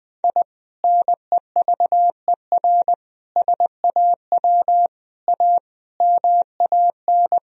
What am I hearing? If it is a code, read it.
Morse code, heard as a keyed tone, about 20 wpm, 700 Hz. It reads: I NEVER SAW A MAN
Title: Sentences from Top 1000 Words - Review Part 3 - 20wpm